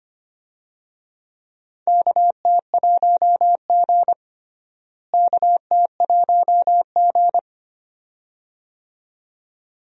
Morse code, heard as a keyed tone, about 25 wpm, 700 Hz. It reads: KT1G KT1G
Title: USA Call Signs - 4 Characters - Repeated 2x - 25wpm